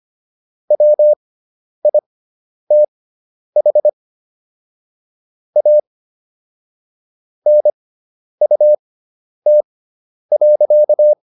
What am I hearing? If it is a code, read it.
Morse code, heard as a keyed tone, about 25 wpm, spaced as for 10 wpm, 600 Hz. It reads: WITH A NUT.